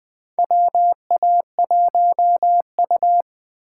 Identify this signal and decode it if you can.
Morse code, heard as a keyed tone, about 20 wpm, 700 Hz. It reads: WA1U
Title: USA Call Signs - 4 Characters - 3x Character Spacing - 20wpm